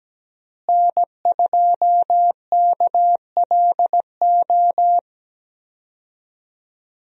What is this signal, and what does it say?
Morse code, heard as a keyed tone, about 17 wpm, 700 Hz. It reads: N2KLO
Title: USA Call Signs - 5 Characters - Repeated 3x - 17wpm